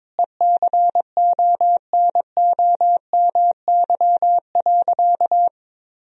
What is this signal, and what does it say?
Morse code, heard as a keyed tone, about 22 wpm, 700 Hz. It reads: ECONOMY.
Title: Sentences from Top 1000 Words - Review Part 3 - 22wpm